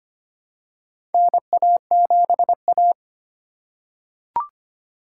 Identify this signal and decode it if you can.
Morse code, heard as a keyed tone, about 25 wpm, 700 Hz. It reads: NA7A
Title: USA Call Signs - 4 Characters - 3x Character Spacing - 25wpm